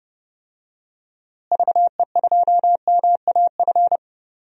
Morse code, heard as a keyed tone, about 32 wpm, 700 Hz. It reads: VE2MAF
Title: DXCC-100 Call Signs - 6 Characters - Repeated 3x - 30wpm